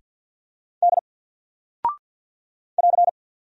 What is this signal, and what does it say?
Morse code, heard as a keyed tone, about 48 wpm, 700 Hz. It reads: D <AR>
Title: Single Character (Letter:Number:Punctuation:Prosign) Alt 50wpm